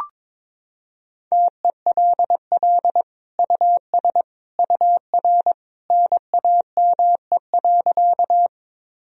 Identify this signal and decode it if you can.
Morse code, heard as a keyed tone, about 22 wpm, 700 Hz. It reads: TELL US UR NAME.
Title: Sentences from Top 200 Words and Abbreviations 22wpm